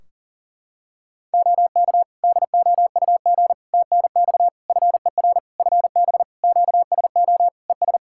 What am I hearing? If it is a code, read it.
Morse code, heard as a keyed tone, about 38 wpm, 700 Hz. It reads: OK DOUG TNX FER FB QSO ES
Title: K1USN SST QSO Practice 40wpm